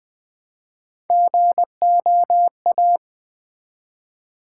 Morse code, heard as a keyed tone, about 20 wpm, 700 Hz. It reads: GOA
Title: Build A Word - 1 Letter At A Time - 4-Letter Words - 20wpm